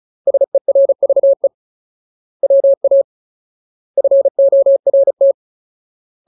Morse code, heard as a keyed tone, about 34 wpm, 550 Hz. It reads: SERVE WA FORT
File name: Ultimate Word List / 35WPM Ultimate